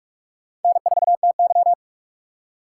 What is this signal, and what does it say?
Morse code, heard as a keyed tone, about 47 wpm, 700 Hz. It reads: N4TY
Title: USA Call Signs - 4 Characters - 4x Character Spacing - 45wpm